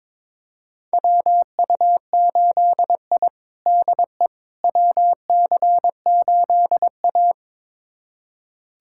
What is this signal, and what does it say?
Morse code, heard as a keyed tone, about 22 wpm, 700 Hz. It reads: WU8I DE WC8A
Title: USA Callsign DE Callsign - 4 Characters - 22wpm